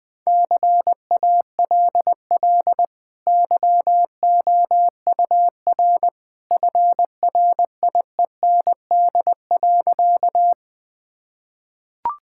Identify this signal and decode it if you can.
Morse code, heard as a keyed tone, about 20 wpm, 700 Hz. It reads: CALL YOUR FRIEND.
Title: Easy Sentences with Hints - 20wpm